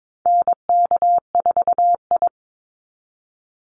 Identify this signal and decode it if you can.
Morse code, heard as a keyed tone, about 22 wpm, 700 Hz. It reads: NK4I
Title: Call Signs Repeated 2x - 22wpm